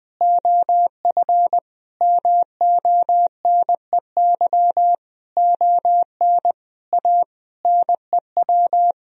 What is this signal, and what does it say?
Morse code, heard as a keyed tone, about 20 wpm, 700 Hz. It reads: OF MONEY ON A NEW
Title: Sentences from Top 1000 Words - Review Part 4 - 20wpm